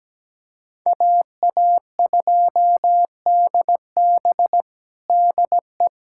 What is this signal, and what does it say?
Morse code, heard as a keyed tone, about 17 wpm, 700 Hz. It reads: AA2DB DE
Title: USA Callsign DE Callsign - 5 Characters - 17wpm